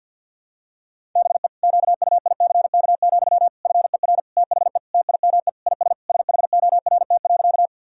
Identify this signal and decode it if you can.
Morse code, heard as a keyed tone, about 49 wpm, 700 Hz. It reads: BE QUICK, FER THE TIME IS SHORT.
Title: Sentences from Top 500 Words and Abbreviations 50wpm